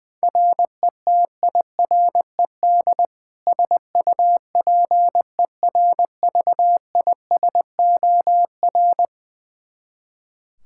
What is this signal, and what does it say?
Morse code, heard as a keyed tone, about 20 wpm, 700 Hz. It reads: RETIRED SUPERVISOR
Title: Practice QSO - Exchange 3 - More Info - 20wpm